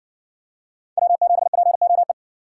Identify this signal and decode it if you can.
Morse code, heard as a keyed tone, about 60 wpm, 700 Hz. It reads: W7POE